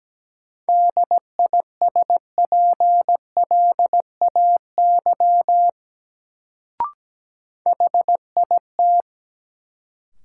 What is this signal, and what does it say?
Morse code, heard as a keyed tone, about 17 wpm, 700 Hz. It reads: DISPLAY HIT
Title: Sentences from Top 1200 Words - New Word Drill Alt - 17wpm